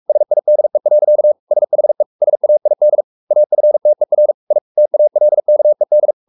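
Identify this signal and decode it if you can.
Morse code, heard as a keyed tone, about 44 wpm, 600 Hz. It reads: SIDE. SHE SAID AFTER I TALKED
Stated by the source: Morse code at 43 wpm sending Ken Follett's novel Never in English, from CW med Gnister.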